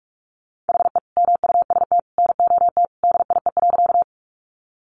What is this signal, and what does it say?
Morse code, heard as a keyed tone, about 44 wpm, 700 Hz. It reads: HE MUST NOT DIE.
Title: Sentences from Top 1300 Words 45wpm